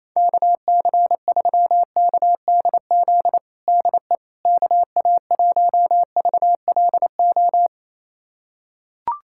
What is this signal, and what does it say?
Morse code, heard as a keyed tone, about 28 wpm, 700 Hz. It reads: KC3KDZ DE KA1VLO
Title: USA Callsign DE Callsign - 6 Characters - 28wpm